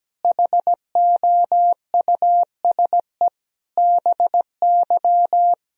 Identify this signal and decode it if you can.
Morse code, heard as a keyed tone, about 17 wpm, 700 Hz. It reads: HOUSE BY
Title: Sentences from Top 1000 Words - Review Part 4 - 17wpm